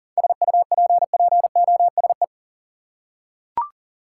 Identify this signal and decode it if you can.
Morse code, heard as a keyed tone, about 38 wpm, 700 Hz. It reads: SUPPOSE
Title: Seven Letter Words - 40wpm